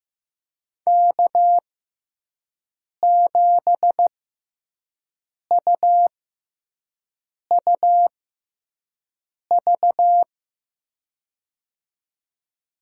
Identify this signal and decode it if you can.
Morse code, heard as a keyed tone, about 15 wpm, spaced as for 5 wpm, 700 Hz. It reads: K7UUV
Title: USA Call Signs - 5 Characters - 6x Character Spacing - 15wpm